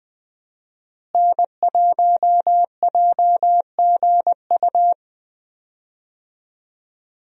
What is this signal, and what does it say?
Morse code, heard as a keyed tone, about 20 wpm, 700 Hz. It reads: N1JGU
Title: USA Call Signs - 5 Characters - Repeated 2x - 20wpm